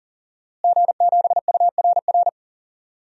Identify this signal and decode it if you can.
Morse code, heard as a keyed tone, about 40 wpm, 700 Hz. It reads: G7URR
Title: DXCC-100 Call Signs - 5 Characters - Repeated 3x - 40wpm